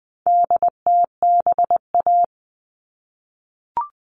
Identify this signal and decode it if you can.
Morse code, heard as a keyed tone, about 20 wpm, 700 Hz. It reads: DTBA
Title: All Cut Numbers - 4-Digit 20wpm